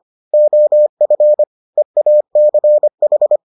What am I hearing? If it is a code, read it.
Morse code, heard as a keyed tone, about 25 wpm, 600 Hz. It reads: OF EACH